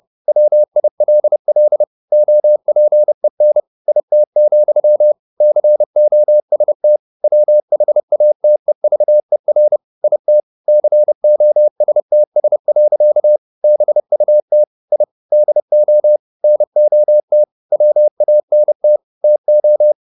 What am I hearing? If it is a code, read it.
Morse code, heard as a keyed tone, about 30 wpm, 600 Hz. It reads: WILL OPEN IT, COST WHATEVER IT COSTS. BUT I DO NOT WANT TO
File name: the old man and the sea / the old man and the sea 1 30wpm